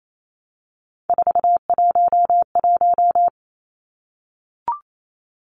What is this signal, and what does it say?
Morse code, heard as a keyed tone, about 28 wpm, 700 Hz. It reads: 411
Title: Numbers - 3-Digit 28wpm